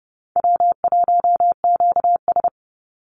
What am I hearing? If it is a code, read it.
Morse code, heard as a keyed tone, about 30 wpm, 700 Hz. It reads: W1QS
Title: Easy Call Signs Repeated 2x - 30wpm